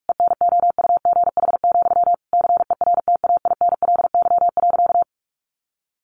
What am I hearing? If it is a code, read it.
Morse code, heard as a keyed tone, about 44 wpm, 700 Hz. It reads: ENOUGH, CERTAINLY.
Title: Sentences from Top 900 Words 45wpm